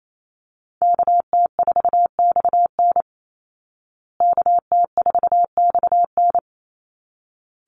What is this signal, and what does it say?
Morse code, heard as a keyed tone, about 28 wpm, 700 Hz. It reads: KT4XN KT4XN
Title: Call Signs Repeated 3x - 28wpm